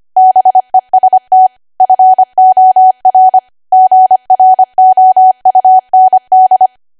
Morse code, heard as a keyed tone, about 25 wpm, 750 Hz. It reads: BEST FOR GROUND
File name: Archive / 25 WPM / 25WPM